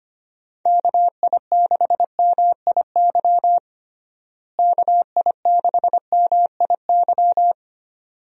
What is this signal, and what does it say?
Morse code, heard as a keyed tone, about 25 wpm, 700 Hz. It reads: KI6MIY KI6MIY
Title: USA Call Signs - 6 Characters - Repeated 2x - 25wpm